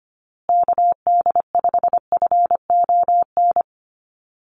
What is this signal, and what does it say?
Morse code, heard as a keyed tone, about 25 wpm, 700 Hz. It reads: KD5FON
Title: USA Call Signs Repeated 3x 25wpm